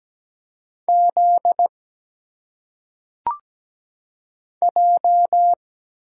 Morse code, heard as a keyed tone, about 17 wpm, 700 Hz. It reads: Z J
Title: Single Letters - Phonetic Pronunciation 17wpm